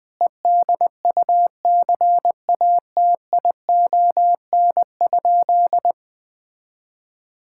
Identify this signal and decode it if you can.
Morse code, heard as a keyed tone, about 21 wpm, 700 Hz. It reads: EDUCATION?
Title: Sentences from Top 1600 Words 20wpm